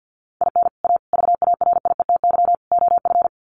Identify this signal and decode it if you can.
Morse code, heard as a keyed tone, about 49 wpm, 700 Hz. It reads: IN A VARIETY OF
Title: Sets of 4 Words 50wpm